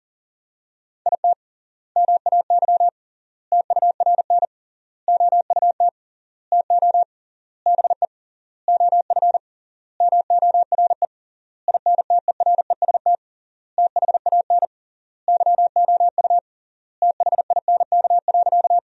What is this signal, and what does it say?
Morse code, heard as a keyed tone, about 37 wpm, 700 Hz. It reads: IT MAY TURN OUT TO BE OF MORE INTEREST THAN YOU THINK.
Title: 3x Word-Spacing - Sentences from Top 400 Words - 40wpm